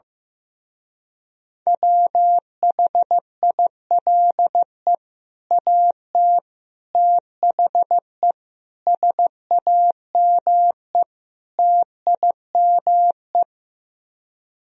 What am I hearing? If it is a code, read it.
Morse code, heard as a keyed tone, about 15 wpm, 700 Hz. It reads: WHILE AT THE SAME TIME
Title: Speed-Racing - Sets of 5 Words 15wpm